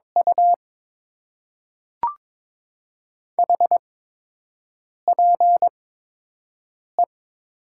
Morse code, heard as a keyed tone, about 22 wpm, spaced as for 6 wpm, 700 Hz. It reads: U HPE
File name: Top 90 QSO Elements - 8x Character Spacing - 22wpm